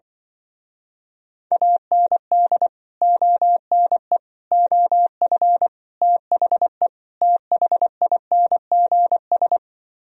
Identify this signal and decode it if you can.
Morse code, heard as a keyed tone, about 24 wpm, 700 Hz. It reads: AND ONE OF THE THINGS
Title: Speed-Racing - Sets of 5 Words 20wpm